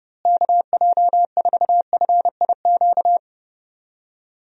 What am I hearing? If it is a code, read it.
Morse code, heard as a keyed tone, about 30 wpm, 700 Hz. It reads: KJ4FIQ
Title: USA Call Signs - 6 Characters - Repeated 3x - 30wpm